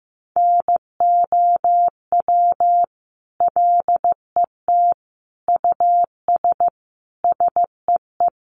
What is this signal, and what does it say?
Morse code, heard as a keyed tone, about 15 wpm, 700 Hz. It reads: NOW LET US SEE
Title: Sentences from Top 1400 Words 15wpm